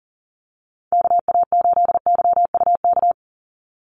Morse code, heard as a keyed tone, about 40 wpm, 700 Hz. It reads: KA8YUK
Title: USA Call Signs 40wpm